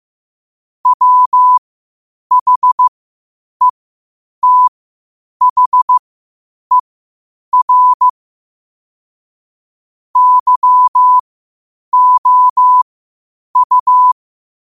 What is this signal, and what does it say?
Morse code, heard as a keyed tone, about 15 wpm, spaced as for 8 wpm, 1000 Hz. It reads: WHETHER YOU
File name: qotdfiles / 8wpm